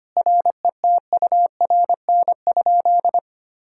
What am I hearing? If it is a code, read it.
Morse code, heard as a keyed tone, about 25 wpm, 700 Hz. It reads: RETURN?
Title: Sentences from Top 1600 Words 25wpm